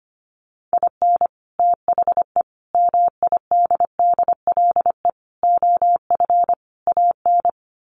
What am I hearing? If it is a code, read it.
Morse code, heard as a keyed tone, about 25 wpm, 700 Hz. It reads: IN THE MIDDLE OF AN
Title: Sets of 5 Words Encore2 25wpm